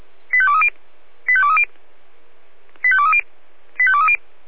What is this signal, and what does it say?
Five-tone selective call (ZVEI), heard as quick runs of five tones, tones of about 70 ms. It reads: 85230 85230 85230 85230